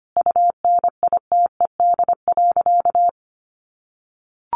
Morse code, heard as a keyed tone, about 25 wpm, 700 Hz. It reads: UNITED.
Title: Sentences from Top 600 Words 25wpm